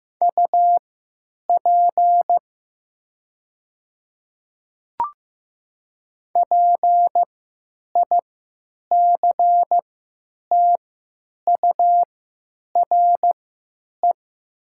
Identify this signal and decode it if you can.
Morse code, heard as a keyed tone, about 15 wpm, spaced as for 9 wpm, 700 Hz. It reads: UP PICTURE
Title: 3x Character Spacing - Top 300 Words - 15wpm